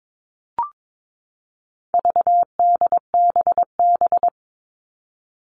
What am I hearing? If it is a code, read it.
Morse code, heard as a keyed tone, about 22 wpm, 700 Hz. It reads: VDBB
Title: All Cut Numbers - 4-Digit 22wpm